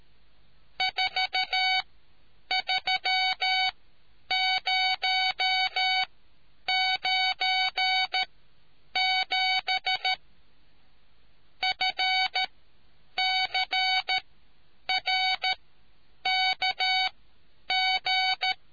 Morse code, heard as a keyed tone, about 13 wpm, spaced as for 8 wpm, 2250 Hz. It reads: 43097 FCRKG